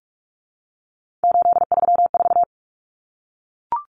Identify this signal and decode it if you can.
Morse code, heard as a keyed tone, about 44 wpm, 700 Hz. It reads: 834
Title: Common Cut Numbers - 3-Digit 45wpm